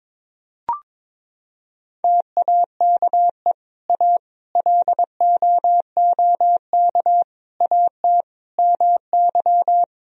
Morse code, heard as a keyed tone, about 22 wpm, 700 Hz. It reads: TAKE A LOOK AT MY
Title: Sentences from Top 1000 Words - Review Part 3 - 22wpm